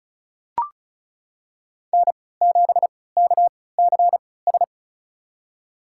Morse code, heard as a keyed tone, about 36 wpm, spaced as for 20 wpm, 700 Hz. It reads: N7KCS
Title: USA Call Signs - 5 Characters - 3x Character Spacing - 35wpm